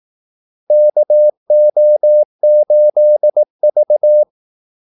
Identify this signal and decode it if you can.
Morse code, heard as a keyed tone, about 18 wpm, 600 Hz. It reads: KO8V
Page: Kaldesignaler 18 wpm